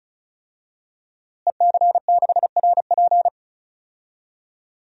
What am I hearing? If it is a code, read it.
Morse code, heard as a keyed tone, about 36 wpm, 700 Hz. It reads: EC6RP
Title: DXCC-100 Call Signs - 5 Characters - Repeated 3x - 35wpm